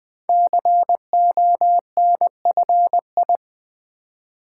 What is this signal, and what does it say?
Morse code, heard as a keyed tone, about 20 wpm, 700 Hz. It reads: CONFI
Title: Build A Word - 5 Letters At A Time - 10-Letter Words - 20wpm